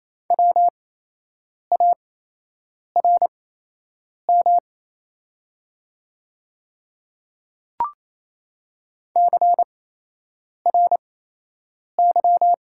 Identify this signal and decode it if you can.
Morse code, heard as a keyed tone, about 28 wpm, spaced as for 8 wpm, 700 Hz. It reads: WARM CRY
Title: 8x Character Spacing - Top 500 Words - 28wpm